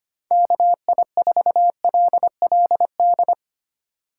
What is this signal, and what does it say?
Morse code, heard as a keyed tone, about 25 wpm, 700 Hz. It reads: KI4LLD
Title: USA Call Signs - 6 Characters - 6x Character Spacing - 25wpm